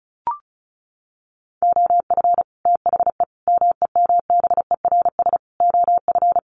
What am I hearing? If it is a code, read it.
Morse code, heard as a keyed tone, about 34 wpm, 700 Hz. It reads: OF THE MEMBERS OF
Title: Sets of 5 Words Encore2 35wpm